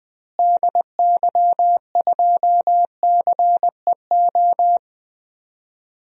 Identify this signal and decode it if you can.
Morse code, heard as a keyed tone, about 20 wpm, 700 Hz. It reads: DY2CEO
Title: DXCC-100 Call Signs - 6 Characters - Repeated 3x - 20wpm